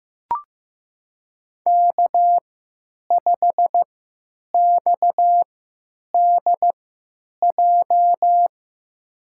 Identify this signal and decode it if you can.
Morse code, heard as a keyed tone, about 15 wpm, spaced as for 9 wpm, 700 Hz. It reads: K5XDJ
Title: USA Call Signs - 5 Characters - 3x Character Spacing - 15wpm